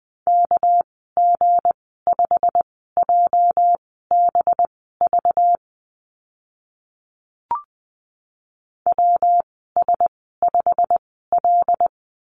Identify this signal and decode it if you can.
Morse code, heard as a keyed tone, about 20 wpm, spaced as for 14 wpm, 700 Hz. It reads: KG5JBV WS5L
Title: 207 Lesson 33g - Letter j - Call Signs 20wpm